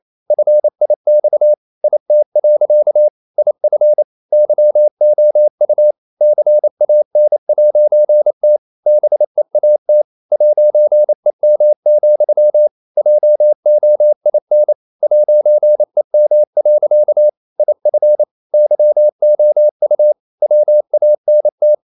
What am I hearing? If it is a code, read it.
Morse code, heard as a keyed tone, about 28 wpm, 600 Hz. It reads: FIX IT. IF YOU CAN'T BEAT 'EM, JOIN 'EM. IF YOU WANT
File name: CW / Practice / EnglishSayings28